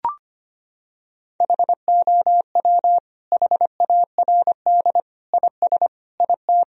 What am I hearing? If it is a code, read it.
Morse code, heard as a keyed tone, about 25 wpm, 700 Hz. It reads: HOW HARD IS IT
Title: Sentences from Top 1000 Words - Review Part 3 - 25wpm